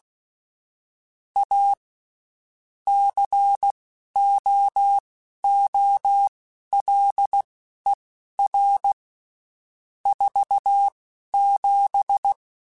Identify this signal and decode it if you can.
Morse code, heard as a keyed tone, about 16 wpm, spaced as for 11 wpm, 800 Hz. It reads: A COOLER 47
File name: Spaced Characters / Spaced Characters-mp3s / Weather 5 5 sp @15